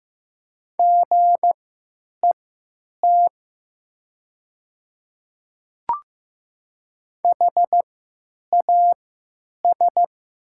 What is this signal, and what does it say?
Morse code, heard as a keyed tone, about 15 wpm, spaced as for 9 wpm, 700 Hz. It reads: GET HAS
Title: 3x Character Spacing - Top 200 Words - 15wpm